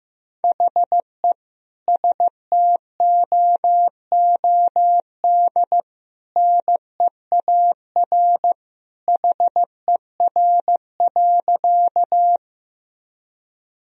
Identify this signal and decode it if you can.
Morse code, heard as a keyed tone, about 15 wpm, 700 Hz. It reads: HE STOOD NEAR HER.
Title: Sentences from Top 1000 Words - Review Part 2 - 15wpm